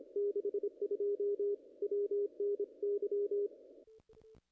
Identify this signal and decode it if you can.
Morse code, heard as a keyed tone, about 25 wpm, 410 Hz. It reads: 62 WNY
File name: contests / ss-cw-k5tr-op-n5rz -0000